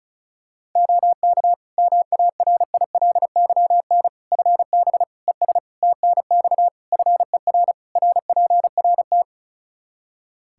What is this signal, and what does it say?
Morse code, heard as a keyed tone, about 34 wpm, 700 Hz. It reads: OK MARILYN FB ES TNX FER RPRT
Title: Practice QSO - Exchange 2 - Info - 35wpm